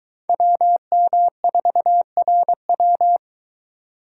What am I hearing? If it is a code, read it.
Morse code, heard as a keyed tone, about 23 wpm, 700 Hz. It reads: WM4RW